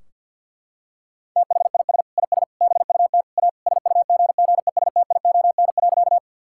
Morse code, heard as a keyed tone, about 50 wpm, 700 Hz. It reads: THIS IS BUT A SUGGESTION.
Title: Sentences from Top 1600 Words 50wpm